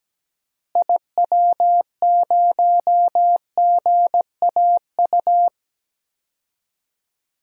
Morse code, heard as a keyed tone, about 17 wpm, 700 Hz. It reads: IW0GAU
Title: DXCC-100 Call Signs - 6 Characters - Repeated 3x - 17wpm